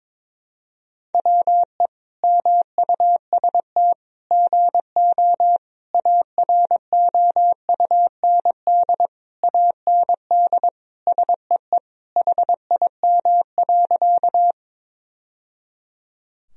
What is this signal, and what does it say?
Morse code, heard as a keyed tone, about 21 wpm, 700 Hz. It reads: WE MUST GO AROUND AND SEE HIM.
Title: Sentences from Top 1000 Words - Review Part 3 - 22wpm